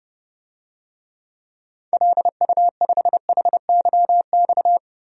Morse code, heard as a keyed tone, about 30 wpm, 700 Hz. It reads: LU5HYX
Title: DXCC-100 Call Signs - 6 Characters - 30wpm